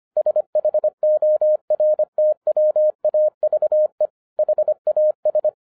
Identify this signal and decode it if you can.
Morse code, heard as a keyed tone, about 25 wpm, 610 Hz. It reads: SHORTWAVE HAS